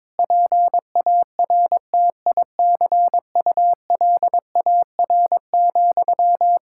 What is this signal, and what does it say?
Morse code, heard as a keyed tone, about 22 wpm, 700 Hz. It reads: PARTICULAR,
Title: Sentences from Top 1000 Words - Review Part 5 - 22wpm